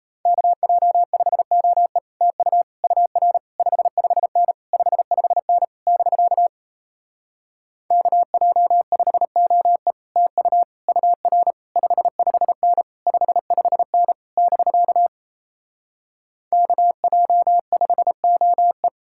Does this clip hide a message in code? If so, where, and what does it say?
Yes: Morse code from 0.3 to 18.9 s, heard as a keyed tone, about 38 wpm, 700 Hz. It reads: KJ5OE TU UR 55N 55N <BK> KJ5OE TU UR 55N 55N <BK> KJ5OE